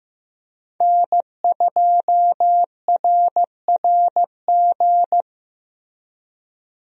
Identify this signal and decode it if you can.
Morse code, heard as a keyed tone, about 15 wpm, 700 Hz. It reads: N2RRG
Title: USA Call Signs - 5 Characters - Repeated 3x - 15wpm